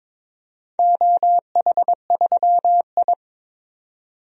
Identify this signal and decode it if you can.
Morse code, heard as a keyed tone, about 22 wpm, 700 Hz. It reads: OH3I
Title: DXCC-100 Call Signs - 4 Characters - 22wpm